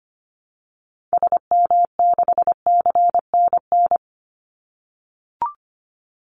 Morse code, heard as a keyed tone, about 25 wpm, 700 Hz. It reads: SM6CNN
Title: Call Signs - 25wpm